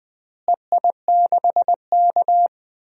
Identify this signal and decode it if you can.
Morse code, heard as a keyed tone, about 20 wpm, 700 Hz. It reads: EI6K
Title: DXCC-100 Call Signs - 4 Characters - Repeated 2x - 20wpm